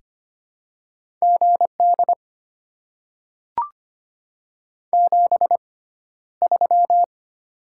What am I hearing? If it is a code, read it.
Morse code, heard as a keyed tone, about 25 wpm, spaced as for 13 wpm, 700 Hz. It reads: GD 73
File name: Top 90 QSO Elements - 6x Character Spacing - 25wpm